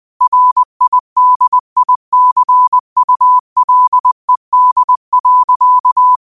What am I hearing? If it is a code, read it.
Morse code, heard as a keyed tone, about 20 wpm, 1000 Hz. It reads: RIDICULED.
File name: qotdfiles / 20wpm